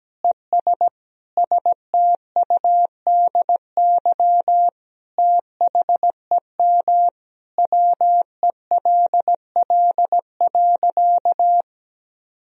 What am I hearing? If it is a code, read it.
Morse code, heard as a keyed tone, about 17 wpm, 700 Hz. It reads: ES STUDY THEM WELL.
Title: Sentences from Top 300 Words and Abbreviations 17wpm